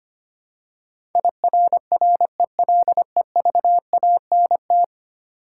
Morse code, heard as a keyed tone, about 25 wpm, 700 Hz. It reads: IRRELEVANT
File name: Ten Letter Words - 25wpm